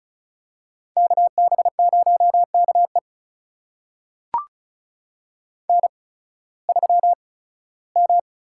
Morse code, heard as a keyed tone, about 35 wpm, spaced as for 18 wpm, 690 Hz. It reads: KB0KE N3M
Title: USA Call Signs - 5 Characters - 8x Character Spacing - 35wpm